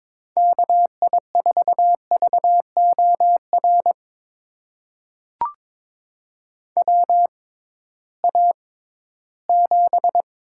Morse code, heard as a keyed tone, about 22 wpm, spaced as for 14 wpm, 700 Hz. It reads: KI4VOR WA7
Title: USA Call Signs - 6 Characters - 6x Character Spacing - 22wpm